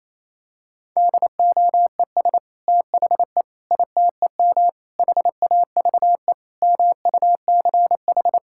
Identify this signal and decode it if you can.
Morse code, heard as a keyed tone, about 28 wpm, 700 Hz. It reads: DOES THE ITEM HAVE MUCH
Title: Sentences from Top 1000 Words - Review Part 6 - 28wpm